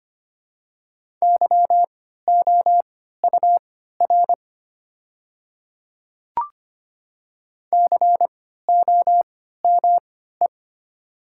Morse code, heard as a keyed tone, about 25 wpm, spaced as for 14 wpm, 700 Hz. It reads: YOUR COME
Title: 3x Character Spacing - Top 100 Words - 25wpm